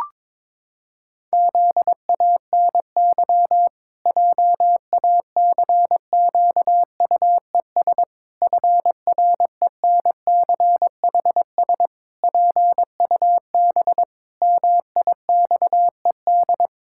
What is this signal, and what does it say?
Morse code, heard as a keyed tone, about 22 wpm, 700 Hz. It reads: ZANY JACQUES FRENCHS PUB MIXED
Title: Pangram Challenge v2 - 22wpm